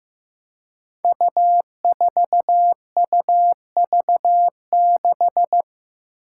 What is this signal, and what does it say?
Morse code, heard as a keyed tone, about 15 wpm, 700 Hz. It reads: U4UV6
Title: Five Challenging Letter-Number Combinations 15wpm